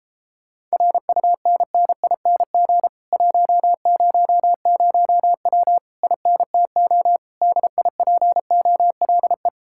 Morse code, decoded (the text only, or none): RUNNING 100W INTO DIPOLE